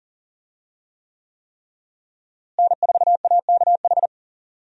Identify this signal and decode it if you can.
Morse code, heard as a keyed tone, about 40 wpm, 700 Hz. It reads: N4AKH
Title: USA Call Signs - 5 Characters - Repeated 3x - 40wpm